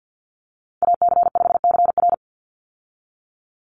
Morse code, heard as a keyed tone, about 50 wpm, 700 Hz. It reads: AC5CR
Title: USA Call Signs 50wpm